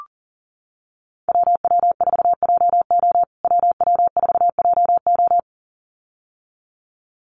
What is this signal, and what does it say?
Morse code, heard as a keyed tone, about 40 wpm, 700 Hz. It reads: WW4JO WW4JO
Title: USA Call Signs Repeated 2x 40wpm